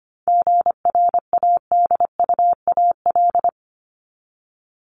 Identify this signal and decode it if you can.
Morse code, heard as a keyed tone, about 25 wpm, 700 Hz. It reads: GRADUAL
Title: LY Suffix - Top 100 Words with Base Words 25wpm